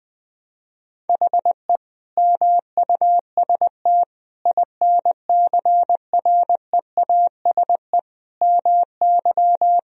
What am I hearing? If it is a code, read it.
Morse code, heard as a keyed tone, about 20 wpm, 700 Hz. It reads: HE MUST INCREASE MY
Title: Sentences from Top 1000 Words - Review Part 4 - 20wpm